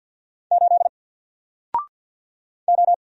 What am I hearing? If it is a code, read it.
Morse code, heard as a keyed tone, about 50 wpm, 700 Hz. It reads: 8 O
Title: Single Character (Letter:Number:Punctuation:Prosign) 50wpm